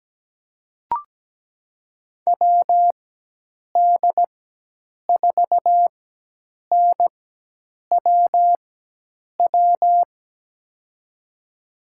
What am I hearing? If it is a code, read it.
Morse code, heard as a keyed tone, about 17 wpm, spaced as for 8 wpm, 700 Hz. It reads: WD4NWW
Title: USA Call Signs - 6 Characters - 4x Character Spacing - 17wpm